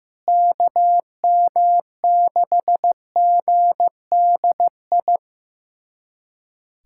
Morse code, heard as a keyed tone, about 15 wpm, 700 Hz. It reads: KM6GDI
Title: USA Callsign DE Callsign - 6 Characters - 15wpm